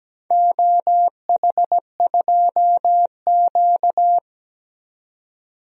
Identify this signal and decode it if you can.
Morse code, heard as a keyed tone, about 17 wpm, 700 Hz. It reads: OH2Q
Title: DXCC-100 Call Signs - 4 Characters - 17wpm